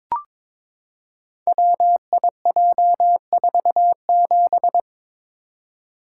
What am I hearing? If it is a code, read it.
Morse code, heard as a keyed tone, about 22 wpm, 700 Hz. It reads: WIJ47
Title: Code Groups - 5 Letter:Numbers - 22wpm